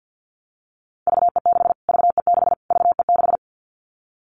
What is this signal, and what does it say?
Morse code, heard as a keyed tone, about 50 wpm, 700 Hz. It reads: VE6 VE6 VE6
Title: Callsign Prefixes for Top 25 DXCC - Repeated 3x - 50wpm